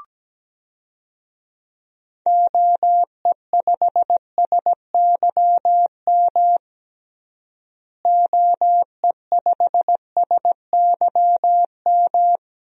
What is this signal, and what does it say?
Morse code, heard as a keyed tone, about 17 wpm, 700 Hz. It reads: OE5SYM OE5SYM
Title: DXCC-100 Call Signs - 6 Characters - Repeated 2x - 17wpm